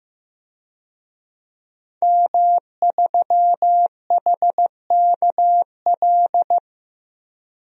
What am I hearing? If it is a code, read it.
Morse code, heard as a keyed tone, about 15 wpm, 700 Hz. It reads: M3HKL
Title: DXCC-100 Call Signs - 5 Characters - Repeated 2x - 15wpm